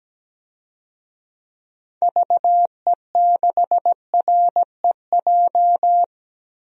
Morse code, heard as a keyed tone, about 17 wpm, 700 Hz. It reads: VE6REJ
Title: DXCC-100 Call Signs - 6 Characters - Repeated 3x - 17wpm